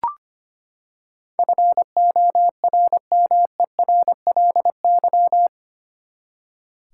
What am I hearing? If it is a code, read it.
Morse code, heard as a keyed tone, about 25 wpm, 700 Hz. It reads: FORMERLY
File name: Eight Letter Words - 25wpm